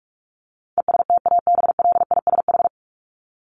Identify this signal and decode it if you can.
Morse code, heard as a keyed tone, about 44 wpm, 700 Hz. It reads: ESTABLISH
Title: ED Suffix - Top 100 Words with Base Words 45wpm